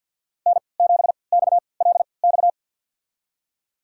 Morse code, heard as a keyed tone, about 53 wpm, spaced as for 29 wpm, 700 Hz. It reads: N7XLX
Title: USA Call Signs - 5 Characters - 3x Character Spacing - 50wpm